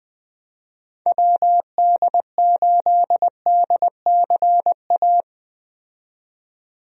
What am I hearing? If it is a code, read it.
Morse code, heard as a keyed tone, about 21 wpm, 700 Hz. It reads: WD8DCA
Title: USA Call Signs - 6 Characters - 20wpm